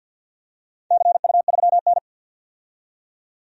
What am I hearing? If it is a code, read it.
Morse code, heard as a keyed tone, about 51 wpm, 700 Hz. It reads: CU3N
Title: DXCC-100 Call Signs - 4 Characters - 50wpm